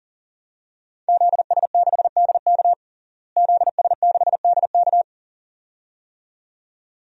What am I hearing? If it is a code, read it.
Morse code, heard as a keyed tone, about 40 wpm, 700 Hz. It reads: ZS6DK ZS6DK
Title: DXCC-100 Call Signs - 5 Characters - Repeated 2x - 40wpm